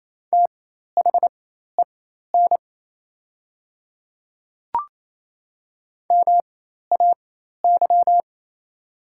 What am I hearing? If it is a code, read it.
Morse code, heard as a keyed tone, about 28 wpm, spaced as for 13 wpm, 700 Hz. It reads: THEN MAY
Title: 4x Character Spacing - Top 300 Words - 28wpm